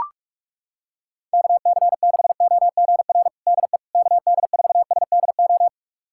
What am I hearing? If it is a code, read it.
Morse code, heard as a keyed tone, about 43 wpm, 700 Hz. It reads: KC6OGR DE KD4SDO
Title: USA Callsign DE Callsign 45wpm